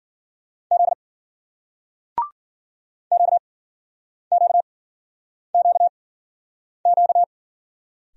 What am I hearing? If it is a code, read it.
Morse code, heard as a keyed tone, about 56 wpm, spaced as for 9 wpm, 700 Hz. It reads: Z QQQQ